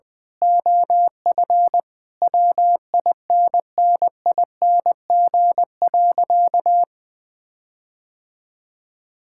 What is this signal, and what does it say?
Morse code, heard as a keyed tone, about 20 wpm, 700 Hz. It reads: OF WINNING.
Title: Sentences with 8-Word Phrases - 20wpm